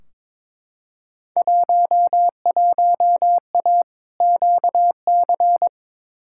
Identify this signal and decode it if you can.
Morse code, heard as a keyed tone, about 21 wpm, 700 Hz. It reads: 11A QC
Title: Field Day Exchanges Encore - 22wpm